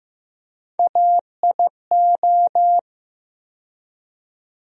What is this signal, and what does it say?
Morse code, heard as a keyed tone, about 15 wpm, 700 Hz. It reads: AIO
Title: USA Call Sign Postfixes - 3 Letters 15wpm